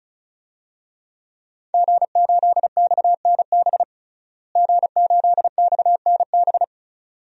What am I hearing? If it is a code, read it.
Morse code, heard as a keyed tone, about 34 wpm, 700 Hz. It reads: G8XNB G8XNB
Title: DXCC-100 Call Signs - 5 Characters - Repeated 2x - 35wpm